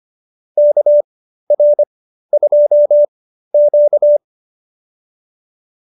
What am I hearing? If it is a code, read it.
Morse code, heard as a keyed tone, about 25 wpm, spaced as for 13 wpm, 590 Hz. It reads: KR2Q